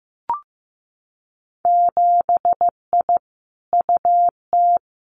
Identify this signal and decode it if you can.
Morse code, heard as a keyed tone, about 15 wpm, 700 Hz. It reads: 7I UT
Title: Winter Field Day Exchanges Encore 15wpm